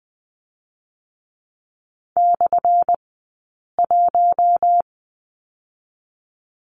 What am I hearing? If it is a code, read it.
Morse code, heard as a keyed tone, about 20 wpm, 700 Hz. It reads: / 1
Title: 177 Lesson 29d - Slash - Two Characters 20wpm